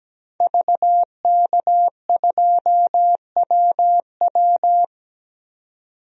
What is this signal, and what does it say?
Morse code, heard as a keyed tone, about 17 wpm, 700 Hz. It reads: VK2WW
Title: DXCC-100 Call Signs - 5 Characters - 17wpm